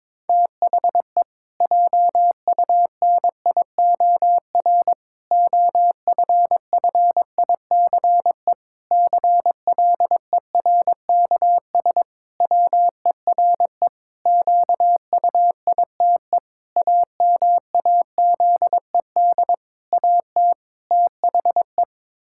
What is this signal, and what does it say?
Morse code, heard as a keyed tone, about 21 wpm, 700 Hz. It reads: THE JUNIOR OFFICE CLERKS WERE QUITE AMAZED AT THE
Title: Pangram Challenge v2 - 22wpm